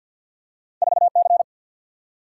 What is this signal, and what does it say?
Morse code, heard as a keyed tone, about 53 wpm, 700 Hz. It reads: 4 C
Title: Single-Letter Single-Number 50wpm